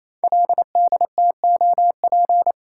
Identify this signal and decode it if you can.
Morse code, heard as a keyed tone, about 28 wpm, 700 Hz. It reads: LDTOP
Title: Code Groups - 5 Letters - 28wpm